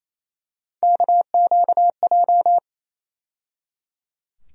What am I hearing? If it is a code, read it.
Morse code, heard as a keyed tone, about 28 wpm, 700 Hz. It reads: KQJ
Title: USA Call Sign Postfixes - 3 Letters 28wpm